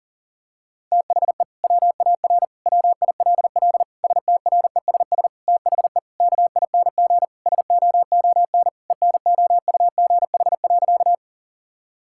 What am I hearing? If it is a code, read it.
Morse code, heard as a keyed tone, about 40 wpm, 700 Hz. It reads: THE WAR WILL STRESS THE KING SOON ENOUGH.
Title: Sentences from Top 1000 Words - Review Part 6 - 40wpm